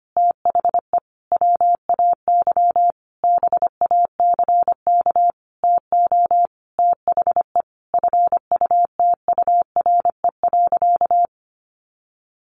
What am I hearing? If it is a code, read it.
Morse code, heard as a keyed tone, about 25 wpm, 700 Hz. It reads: THE WAY BACK TO THE FUTURE.
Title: A Sound Of Thunder 25wpm